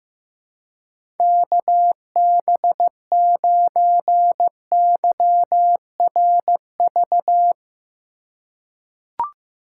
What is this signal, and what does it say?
Morse code, heard as a keyed tone, about 15 wpm, 700 Hz. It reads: KB9YRV
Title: USA Call Signs - 3x Character Spacing - 15wpm